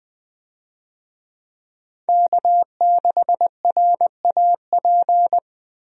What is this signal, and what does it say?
Morse code, heard as a keyed tone, about 21 wpm, 700 Hz. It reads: K6RAP
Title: USA Call Signs - 5 Characters - 20wpm